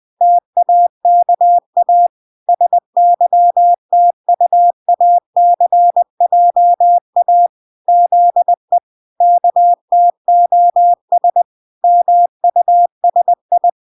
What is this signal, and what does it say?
Morse code, heard as a keyed tone, about 20 wpm, 700 Hz. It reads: TAKA SYTUACJA ZE KTOS MUSI